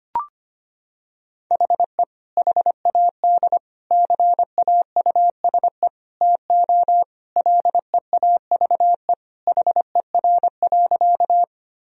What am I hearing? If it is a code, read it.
Morse code, heard as a keyed tone, about 25 wpm, 700 Hz. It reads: HE HAD CAUSE TO LEAVE HER.
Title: Sentences from Top 1000 Words - Review Part 1 - 25wpm